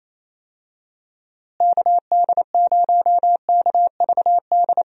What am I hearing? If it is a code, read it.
Morse code, heard as a keyed tone, about 27 wpm, 700 Hz. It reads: KD0KVD
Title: USA Call Signs - 6 Characters - Repeated 2x - 28wpm